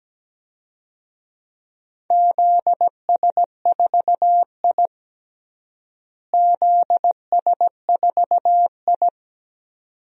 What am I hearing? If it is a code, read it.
Morse code, heard as a keyed tone, about 17 wpm, 700 Hz. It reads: ZS4I ZS4I
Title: DXCC-100 Call Signs - 4 Characters - Repeated 2x - 17wpm